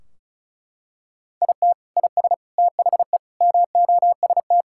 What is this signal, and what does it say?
Morse code, heard as a keyed tone, about 36 wpm, 700 Hz. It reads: IT IS THE MOST
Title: Sentences from Top 1600 Words 35wpm